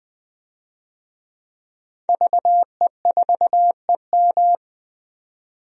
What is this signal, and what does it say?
Morse code, heard as a keyed tone, about 20 wpm, 700 Hz. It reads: VE4EM
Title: DXCC-100 Call Signs - 5 Characters - 20wpm